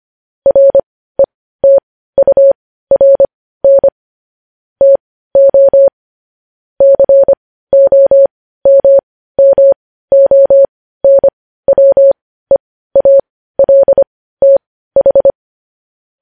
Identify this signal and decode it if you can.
Morse code, heard as a keyed tone, about 25 wpm, spaced as for 15 wpm, 550 Hz. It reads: RETURN TO COMMONWEALTH
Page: news headlines in morse code at 15 wpm